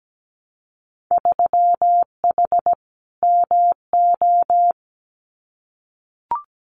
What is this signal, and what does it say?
Morse code, heard as a keyed tone, about 17 wpm, 700 Hz. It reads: 3H MO
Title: Winter Field Day Exchanges Encore 17wpm